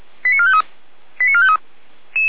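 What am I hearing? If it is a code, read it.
Five-tone selective call (ZVEI), heard as quick runs of five tones, tones of about 70 ms. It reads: 89453 89453
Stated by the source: ZVEI five-tone selective call received on an SDR.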